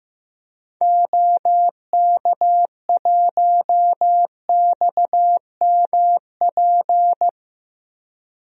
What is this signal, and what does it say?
Morse code, heard as a keyed tone, about 15 wpm, 700 Hz. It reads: OK1XMP
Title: DXCC-100 Call Signs - 6 Characters - 15wpm